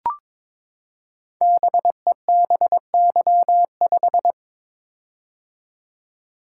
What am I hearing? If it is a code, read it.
Morse code, heard as a keyed tone, about 22 wpm, 700 Hz. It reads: BEBY5
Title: Code Groups - 5 Letter:Numbers - 22wpm